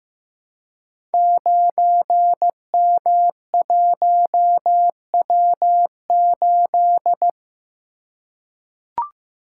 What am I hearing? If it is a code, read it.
Morse code, heard as a keyed tone, about 15 wpm, 700 Hz. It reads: 9M1W8
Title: Code Groups - 5 Letter:Numbers - 15wpm